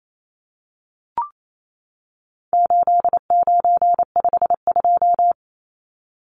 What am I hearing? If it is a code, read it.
Morse code, heard as a keyed tone, about 28 wpm, 700 Hz. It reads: 8952
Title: Numbers - 4-Digit 28wpm